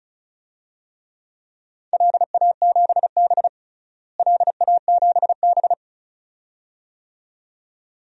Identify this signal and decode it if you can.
Morse code, heard as a keyed tone, about 34 wpm, 700 Hz. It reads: LA7B LA7B
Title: DXCC-100 Call Signs - 4 Characters - Repeated 2x - 35wpm